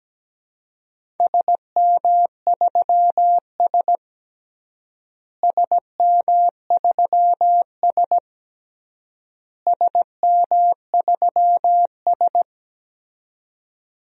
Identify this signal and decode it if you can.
Morse code, heard as a keyed tone, about 17 wpm, 700 Hz. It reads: SM3S SM3S SM3S
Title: DXCC-100 Call Signs - 4 Characters - Repeated 3x - 17wpm